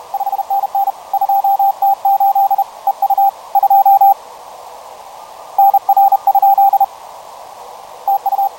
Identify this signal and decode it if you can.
Morse code, heard as a keyed tone, about 31 wpm, 790 Hz, with radio noise. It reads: 5NN 2T8 EU 2 NR? TU
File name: 20m CW